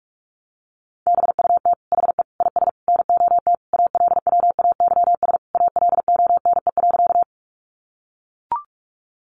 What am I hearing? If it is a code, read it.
Morse code, heard as a keyed tone, about 44 wpm, 700 Hz. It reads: BUT HE IS NOT ALWAYS ALONE.
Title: Sentences from Top 1000 Words 45wpm